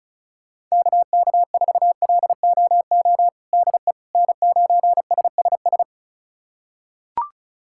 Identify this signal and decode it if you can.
Morse code, heard as a keyed tone, about 36 wpm, 700 Hz. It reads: KK4LOO DE N9SSS
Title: USA Callsign DE Callsign 35wpm